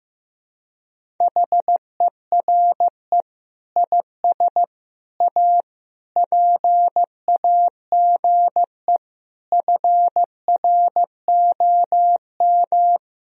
Morse code, heard as a keyed tone, about 15 wpm, 700 Hz. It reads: HERE IS A PAGE FROM
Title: Sentences from Top 1000 Words - Review Part 1 - 15wpm